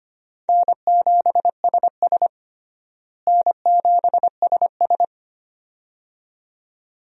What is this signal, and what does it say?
Morse code, heard as a keyed tone, about 25 wpm, 700 Hz. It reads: N7SS N7SS
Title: USA Call Signs - 4 Characters - Repeated 2x - 25wpm